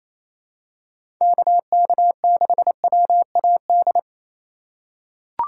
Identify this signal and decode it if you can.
Morse code, heard as a keyed tone, about 28 wpm, 700 Hz. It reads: KK6WAD
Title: USA Call Signs - 6 Characters - 2x Character Spacing - 28wpm